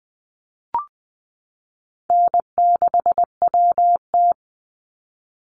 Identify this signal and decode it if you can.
Morse code, heard as a keyed tone, about 20 wpm, 700 Hz. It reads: N6WT
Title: NA SOTA Chaser Callsigns 20wpm